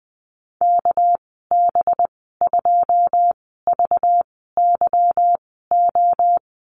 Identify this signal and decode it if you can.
Morse code, heard as a keyed tone, about 20 wpm, spaced as for 14 wpm, 700 Hz. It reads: KB2VYO
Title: 238 Lesson 38g - Letter q - Call Signs 20wpm